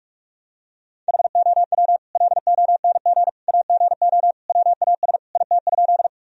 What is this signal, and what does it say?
Morse code, heard as a keyed tone, about 46 wpm, 700 Hz. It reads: HOW LONG AGO WAS IT?
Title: Sentences from Top 1000 Words - Review Part 3 - 45wpm